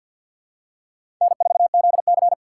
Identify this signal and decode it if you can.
Morse code, heard as a keyed tone, about 48 wpm, 700 Hz. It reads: N4ZC
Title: USA Call Signs - 4 Characters - 2x Character Spacing - 50wpm